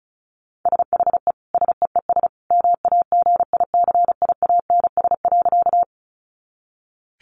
Morse code, heard as a keyed tone, about 34 wpm, 700 Hz. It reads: SHE SEES MAGICIANS.
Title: Easy Phrases 35wpm